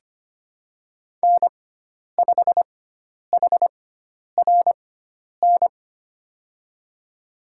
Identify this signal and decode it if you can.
Morse code, heard as a keyed tone, about 25 wpm, spaced as for 10 wpm, 700 Hz. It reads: N5HRN
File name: Lesson 16g - Letter c - Call Signs 25wpm